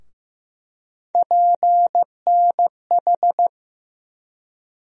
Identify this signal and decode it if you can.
Morse code, heard as a keyed tone, about 15 wpm, 700 Hz. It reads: PNH
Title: USA Call Sign Postfixes - 3 Letters 15wpm